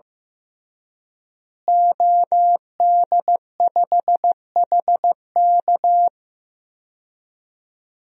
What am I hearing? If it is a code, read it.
Morse code, heard as a keyed tone, about 15 wpm, 700 Hz. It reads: OD5HK
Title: DXCC-100 Call Signs - 5 Characters - Repeated 3x - 15wpm